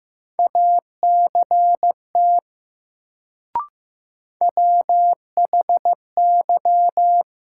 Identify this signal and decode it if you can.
Morse code, heard as a keyed tone, about 15 wpm, 700 Hz. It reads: ACT WHY
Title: Sentences from Top 200 Words - New Word Drill - 15wpm